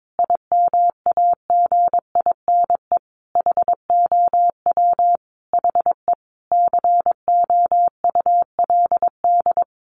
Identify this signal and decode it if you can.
Morse code, heard as a keyed tone, about 22 wpm, 700 Hz. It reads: IMAGINE HOW HE COULD
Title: Sentences from Top 1400 Words 22wpm